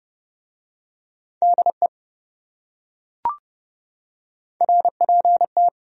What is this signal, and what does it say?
Morse code, heard as a keyed tone, about 27 wpm, 700 Hz. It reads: DE RPT